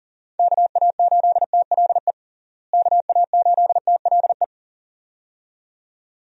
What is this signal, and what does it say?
Morse code, heard as a keyed tone, about 40 wpm, 700 Hz. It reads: KA8TLE KA8TLE
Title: USA Call Signs - 6 Characters - Repeated 2x - 40wpm